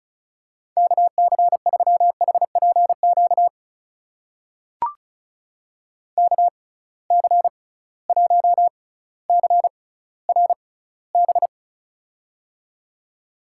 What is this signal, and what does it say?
Morse code, heard as a keyed tone, about 36 wpm, spaced as for 18 wpm, 700 Hz. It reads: KC3HPQ KC1CRB
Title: USA Call Signs - 6 Characters - 6x Character Spacing - 35wpm